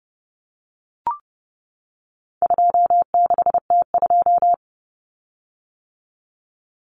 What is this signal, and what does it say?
Morse code, heard as a keyed tone, about 30 wpm, 700 Hz. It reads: 26T2
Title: Common Cut Numbers - 4-Digit 30wpm